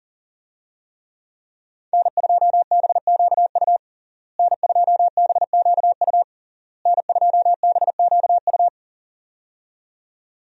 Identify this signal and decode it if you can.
Morse code, heard as a keyed tone, about 40 wpm, 700 Hz. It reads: N2BQU N2BQU N2BQU
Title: USA Call Signs - 5 Characters - Repeated 3x - 40wpm